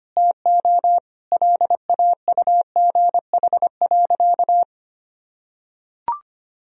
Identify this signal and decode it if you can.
Morse code, heard as a keyed tone, about 25 wpm, 700 Hz. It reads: TO LAUGH.
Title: Sentences from Top 1000 Words - Review Part 1 - 25wpm